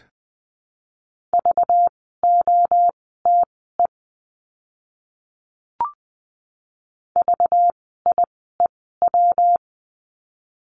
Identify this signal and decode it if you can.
Morse code, heard as a keyed tone, about 19 wpm, spaced as for 14 wpm, 700 Hz. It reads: VOTE VIEW
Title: 192 Lesson 31f - Letter v - Words 20wpm